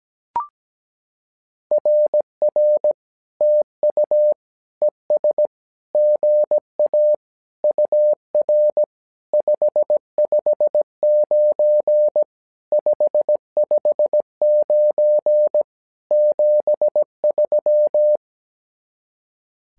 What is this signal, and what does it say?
Morse code, heard as a keyed tone, about 17 wpm, 600 Hz. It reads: RR TU ES GA UR 559 559 73
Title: SOTA QSO Practice 17wpm